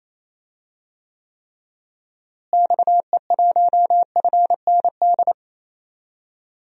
Morse code, heard as a keyed tone, about 28 wpm, 700 Hz. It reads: XE1FND
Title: DXCC-100 Call Signs - 6 Characters - Repeated 3x - 28wpm